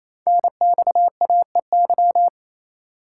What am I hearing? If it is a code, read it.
Morse code, heard as a keyed tone, about 27 wpm, 700 Hz. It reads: NXAEY
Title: Code Groups - 5 Letters - 28wpm